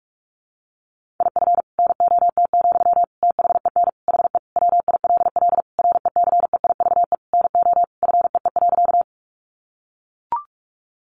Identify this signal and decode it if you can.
Morse code, heard as a keyed tone, about 44 wpm, 700 Hz. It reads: IF NOT, THEN HE WILL RECEIVE NO FEE.
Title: Sentences from Top 1500 Words 45wpm